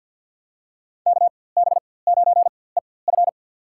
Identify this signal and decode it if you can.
Morse code, heard as a keyed tone, about 49 wpm, spaced as for 23 wpm, 700 Hz. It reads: KB9EF
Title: USA Call Signs - 5 Characters - 4x Character Spacing - 50wpm